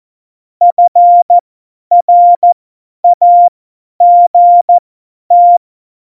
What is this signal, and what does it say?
Morse code, heard as a keyed tone, about 14 wpm, spaced as for 10 wpm, 700 Hz. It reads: FRAGT
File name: Morsekurs / Witze / Witz 26. Dezember 2017, 50 F70